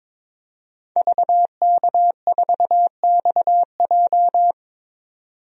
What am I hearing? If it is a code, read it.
Morse code, heard as a keyed tone, about 22 wpm, 700 Hz. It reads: VK4XJ
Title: DXCC-100 Call Signs - 5 Characters - 22wpm